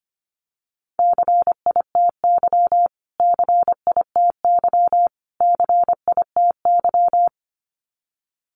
Morse code, heard as a keyed tone, about 25 wpm, 700 Hz. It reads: CITY CITY CITY
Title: Top 500 Words Repeated 25wpm